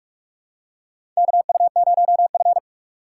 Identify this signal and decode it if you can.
Morse code, heard as a keyed tone, about 48 wpm, 700 Hz. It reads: KU0F
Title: USA Call Signs - 4x Character Spacing - 45wpm